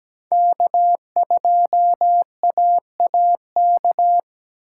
Morse code, heard as a keyed tone, about 17 wpm, 700 Hz. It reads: K2AAK
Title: USA Call Signs - 5 Characters - 4x Character Spacing - 17wpm